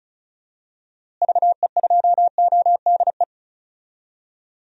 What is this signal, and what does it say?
Morse code, heard as a keyed tone, about 34 wpm, 700 Hz. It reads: VE2ODE
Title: DXCC-100 Call Signs - 6 Characters - Repeated 2x - 35wpm